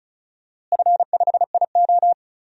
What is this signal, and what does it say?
Morse code, heard as a keyed tone, about 36 wpm, 700 Hz. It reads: F5IO
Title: DXCC-100 Call Signs - 4 Characters - Repeated 3x - 35wpm